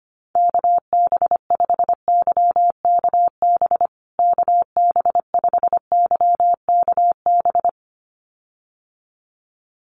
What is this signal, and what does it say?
Morse code, heard as a keyed tone, about 25 wpm, 700 Hz. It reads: KB5YKB KB5YKB
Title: USA Call Signs Repeated 2x 25wpm